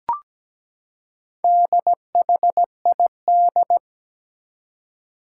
Morse code, heard as a keyed tone, about 17 wpm, 700 Hz. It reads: DHID
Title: Four Challenging Letter-Number Combinations 17wpm